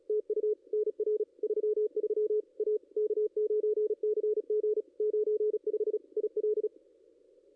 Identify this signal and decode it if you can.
Morse code, heard as a keyed tone, about 36 wpm, 430 Hz. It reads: TU NR 33 A K9CG 95 IL
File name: contests / ss-cw-k5tr-op-n5rz -2200